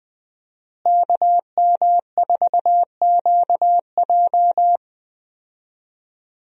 Morse code, heard as a keyed tone, about 20 wpm, 700 Hz. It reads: KM4QJ
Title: USA Call Signs - 5 Characters - Repeated 2x - 20wpm